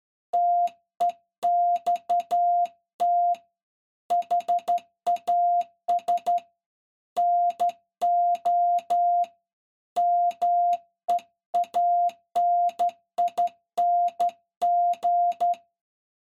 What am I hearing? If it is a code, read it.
Morse code, heard as a keyed tone, about 12 wpm, 700 Hz. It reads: TEXT HAS NO MEANING
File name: se / pre2 / Morse Code01-4(Slow-Key Noise)